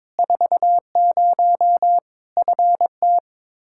Morse code, heard as a keyed tone, about 22 wpm, 700 Hz. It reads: 40 FT
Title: Practice QSO - Exchange 2 - Info - 22wpm